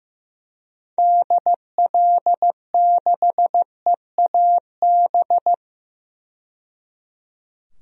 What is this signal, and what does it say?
Morse code, heard as a keyed tone, about 15 wpm, 700 Hz. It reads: DL6EAB
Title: DXCC-100 Call Signs - 6 Characters - 15wpm